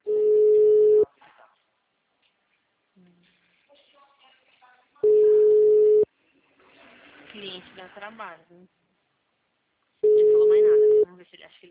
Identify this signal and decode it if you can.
Ringback tone (Europe, 425 Hz), with speech.
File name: out- -2014- - - .637